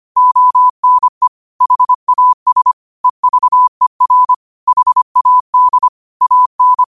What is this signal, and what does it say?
Morse code, heard as a keyed tone, about 25 wpm, 1000 Hz. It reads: ONE HAS EVER HAD AN
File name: 25wpm